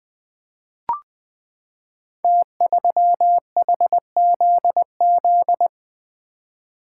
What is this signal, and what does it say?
Morse code, heard as a keyed tone, about 20 wpm, 700 Hz. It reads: T3HZZ
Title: Code Groups - 5 Letter:Numbers - 20wpm